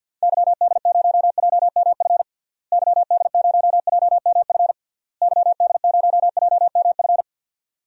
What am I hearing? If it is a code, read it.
Morse code, heard as a keyed tone, about 47 wpm, 700 Hz. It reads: YD0JMF YD0JMF YD0JMF
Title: DXCC-100 Call Signs - 6 Characters - Repeated 3x - 50wpm